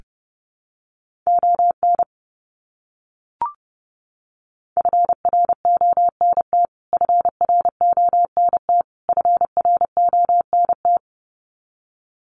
Morse code, heard as a keyed tone, about 30 wpm, 700 Hz. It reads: ON FRONT FRONT FRONT
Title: Top 500 Words Repeated 30wpm